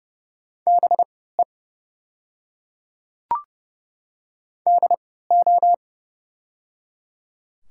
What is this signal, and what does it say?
Morse code, heard as a keyed tone, about 30 wpm, spaced as for 17 wpm, 700 Hz. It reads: BE DO
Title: 3x Character Spacing - Top 200 Words - 30wpm